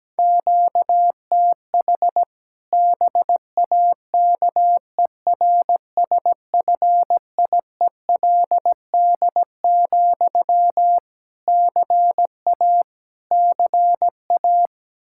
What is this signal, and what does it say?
Morse code, heard as a keyed tone, about 17 wpm, 700 Hz. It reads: QTH BAKERSFIELD, CA CA
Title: Practice QSO - Exchange 1 - Beginning - 17wpm